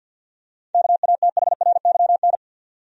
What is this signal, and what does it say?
Morse code, heard as a keyed tone, about 49 wpm, 700 Hz. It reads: KATHRYN
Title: USA Common Names - 50wpm